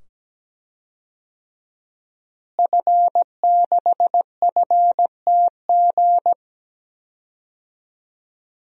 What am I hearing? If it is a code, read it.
Morse code, heard as a keyed tone, about 17 wpm, 700 Hz. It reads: F6FTG
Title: DXCC-100 Call Signs - 5 Characters - Repeated 2x - 17wpm